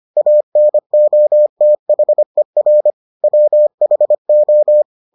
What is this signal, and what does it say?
Morse code, heard as a keyed tone, about 25 wpm, 600 Hz. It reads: ANOTHER WHO